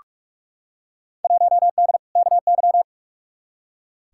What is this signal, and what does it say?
Morse code, heard as a keyed tone, about 43 wpm, 700 Hz. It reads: 1D KY
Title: Field Day Exchanges - 45wpm